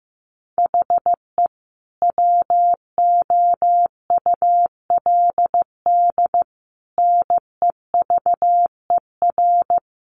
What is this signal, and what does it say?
Morse code, heard as a keyed tone, about 15 wpm, 700 Hz. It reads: HE WOULD NEVER
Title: Sentences from Top 1200 Words 15wpm